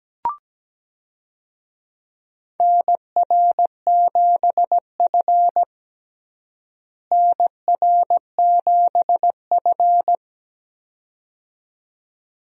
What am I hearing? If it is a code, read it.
Morse code, heard as a keyed tone, about 17 wpm, 700 Hz. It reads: NR7F NR7F
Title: USA Call Signs - 4 Characters - Repeated 2x - 17wpm